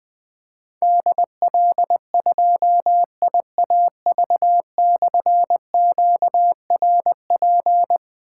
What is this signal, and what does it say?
Morse code, heard as a keyed tone, about 20 wpm, 700 Hz. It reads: DL2IAV/QRP
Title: Difficult Call Signs 20wpm